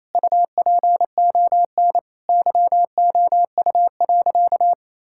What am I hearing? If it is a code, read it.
Morse code, heard as a keyed tone, about 28 wpm, 700 Hz. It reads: UPON YOU.
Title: Sentences from Top 1600 Words 28wpm